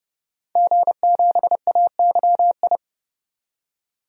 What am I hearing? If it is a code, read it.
Morse code, heard as a keyed tone, about 31 wpm, 700 Hz. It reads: G7AYI
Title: DXCC-100 Call Signs - 5 Characters - 30wpm